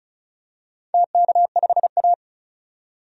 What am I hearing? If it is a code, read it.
Morse code, heard as a keyed tone, about 36 wpm, 700 Hz. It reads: TK5A
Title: DXCC-100 Call Signs - 4 Characters - 35wpm